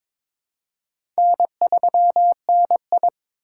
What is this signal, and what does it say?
Morse code, heard as a keyed tone, about 21 wpm, 700 Hz. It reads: N3NI
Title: USA Call Signs - 4 Characters - 4x Character Spacing - 22wpm